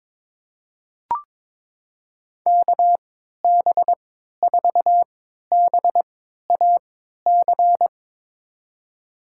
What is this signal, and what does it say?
Morse code, heard as a keyed tone, about 21 wpm, spaced as for 12 wpm, 700 Hz. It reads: KB4BAC
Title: USA Call Signs - 6 Characters - 3x Character Spacing - 22wpm